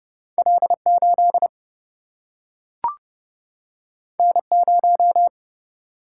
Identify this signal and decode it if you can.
Morse code, heard as a keyed tone, about 30 wpm, 700 Hz. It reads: L 8 N 0
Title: Single-Letter Single-Number 30wpm